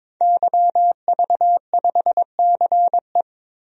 Morse code, heard as a keyed tone, about 22 wpm, 700 Hz. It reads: YV5CE
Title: DXCC-100 Call Signs - 5 Characters - 22wpm